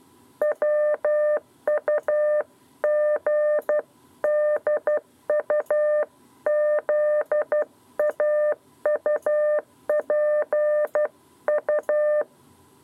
Morse code, heard as a keyed tone, about 12 wpm, 600 Hz. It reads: WUGDU ZAUPU
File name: 7b lezione cw 2